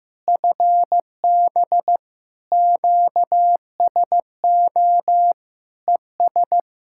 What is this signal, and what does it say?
Morse code, heard as a keyed tone, about 16 wpm, 700 Hz. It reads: FB QSO ES
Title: Practice QSO - Exchange 4 - Ending - 15wpm